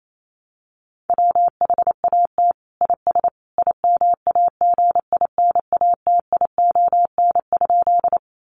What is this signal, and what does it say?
Morse code, heard as a keyed tone, about 28 wpm, 700 Hz. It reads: WHAT IS IMAGINATION?
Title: Sentences from Top 1300 Words 28wpm